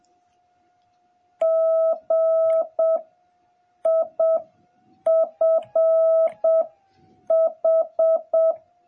Morse code, heard as a keyed tone, about 7 wpm, 650 Hz. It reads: G I F H